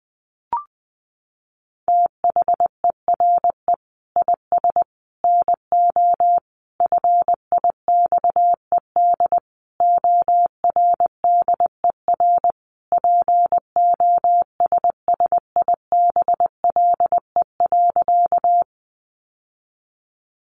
Morse code, heard as a keyed tone, about 20 wpm, 700 Hz. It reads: THERE IS NO FIXED ORDER POSSIBLE.
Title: Sentences from Top 1100 Words 20wpm